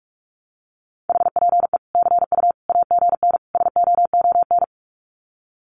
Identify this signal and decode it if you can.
Morse code, heard as a keyed tone, about 45 wpm, 700 Hz. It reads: HPE CU AGN SOON
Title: Common QSO Phrases 45wpm